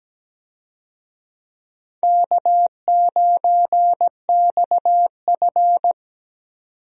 Morse code, heard as a keyed tone, about 17 wpm, 700 Hz. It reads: K9XF
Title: USA Call Signs - 4 Characters - 17wpm